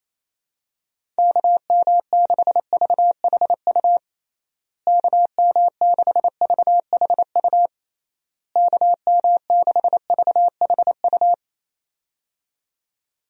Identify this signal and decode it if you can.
Morse code, heard as a keyed tone, about 28 wpm, 700 Hz. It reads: KM6VHU KM6VHU KM6VHU
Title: USA Call Signs - 6 Characters - Repeated 3x - 28wpm